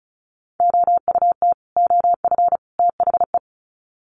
Morse code, heard as a keyed tone, about 34 wpm, 700 Hz. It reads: OUT OF THE
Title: Sets of 3 Words 35wpm